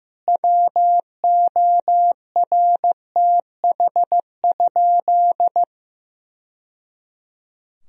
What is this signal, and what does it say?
Morse code, heard as a keyed tone, about 16 wpm, 700 Hz. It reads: WORTH?
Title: Sentences from Top 1000 Words - Review Part 4 - 15wpm